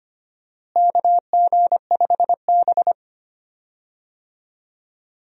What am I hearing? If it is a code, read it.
Morse code, heard as a keyed tone, about 25 wpm, 700 Hz. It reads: KG5B
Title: USA Call Signs - 4 Characters - Repeated 2x - 25wpm